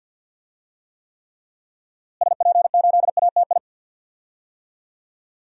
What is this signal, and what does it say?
Morse code, heard as a keyed tone, about 52 wpm, 700 Hz. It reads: SP8ATI
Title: DXCC-100 Call Signs - 6 Characters - Repeated 3x - 50wpm